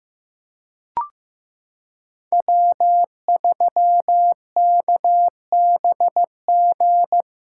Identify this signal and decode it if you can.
Morse code, heard as a keyed tone, about 15 wpm, 700 Hz. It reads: W3KBG
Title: USA Callsign DE Callsign - 5 Characters - 15wpm